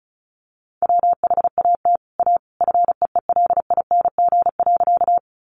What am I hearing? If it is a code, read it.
Morse code, heard as a keyed tone, about 34 wpm, 700 Hz. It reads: WHAT A FEELING.
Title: Sentences from Top 1200 Words 35wpm